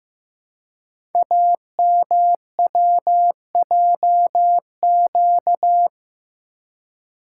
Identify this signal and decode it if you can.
Morse code, heard as a keyed tone, about 15 wpm, 700 Hz. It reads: AMWJQ
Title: Code Groups - 5 Letters - 15wpm